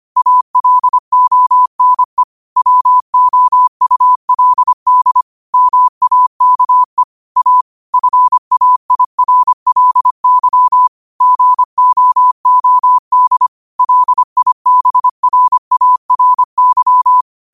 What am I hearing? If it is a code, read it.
Morse code, heard as a keyed tone, about 25 wpm, 1000 Hz. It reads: ALONE WOULD MAKE A FAIRLY GOOD LIBRARY
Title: 25 WPM morse code quotes for Thu, 29 Feb 2024